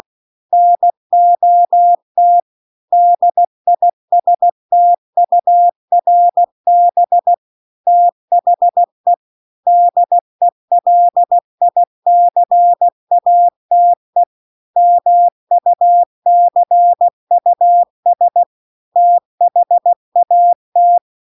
Morse code, heard as a keyed tone, about 16 wpm, 700 Hz. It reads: NOT DISTURB THE DELICATE MUCUS THAT
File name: CW-The-Big-Twoheaded-River-2-16WPM